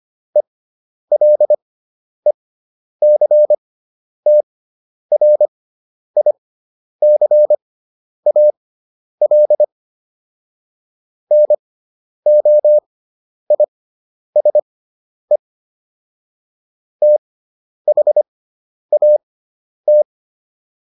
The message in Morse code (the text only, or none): ELECTRICAL NOISE THAT